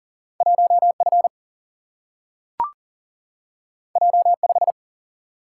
Morse code, heard as a keyed tone, about 40 wpm, 700 Hz. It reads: 1 F J 5
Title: Single-Letter Single-Number 40wpm